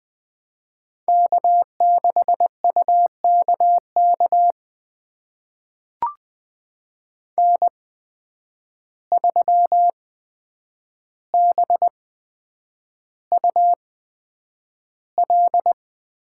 Morse code, heard as a keyed tone, about 20 wpm, spaced as for 9 wpm, 700 Hz. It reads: K6UKK N3BUL
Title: USA Call Signs - 5 Characters - 8x Character Spacing - 20wpm